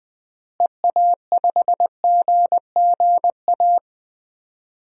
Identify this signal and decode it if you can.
Morse code, heard as a keyed tone, about 20 wpm, 700 Hz. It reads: EA5GGA
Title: DXCC-100 Call Signs - 6 Characters - Repeated 3x - 20wpm